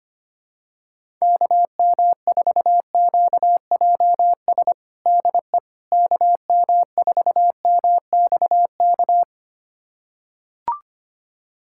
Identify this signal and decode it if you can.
Morse code, heard as a keyed tone, about 25 wpm, 700 Hz. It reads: KM4QJS DE KM4MXK
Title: USA Callsign DE Callsign - 6 Characters - 25wpm